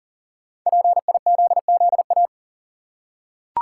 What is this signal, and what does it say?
Morse code, heard as a keyed tone, about 40 wpm, 700 Hz. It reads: PIZZA
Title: Five Letter Words - 40wpm